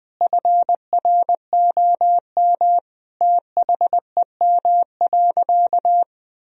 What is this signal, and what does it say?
Morse code, heard as a keyed tone, about 20 wpm, 700 Hz. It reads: FROM THEM.
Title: Sentences from Top 1000 Words - Review Part 3 - 20wpm